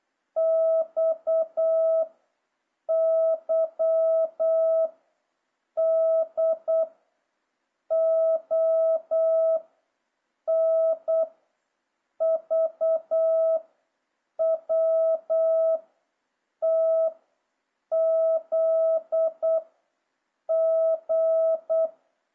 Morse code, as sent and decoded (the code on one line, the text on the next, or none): -..- / -.-- / -.. / --- / -. / ...- / .-- / - / --.. / --.
X Y D O N V W T Z G